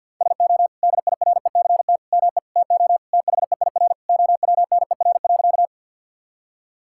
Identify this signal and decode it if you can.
Morse code, heard as a keyed tone, about 50 wpm, 700 Hz. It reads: SO DIRECT ME TO THEIR OWNER.
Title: Sentences from Top 1600 Words 50wpm